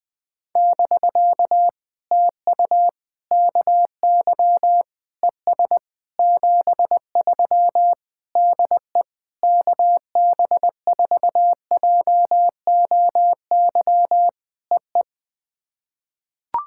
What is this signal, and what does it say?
Morse code, heard as a keyed tone, about 20 wpm, 700 Hz. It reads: <BK> TU KY ES 73 DE KB4JOY EE
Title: POTA QSO Practice - 20wpm